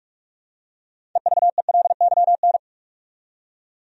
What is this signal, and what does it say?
Morse code, heard as a keyed tone, about 46 wpm, 700 Hz. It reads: EVELYN
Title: USA Common Names - 45wpm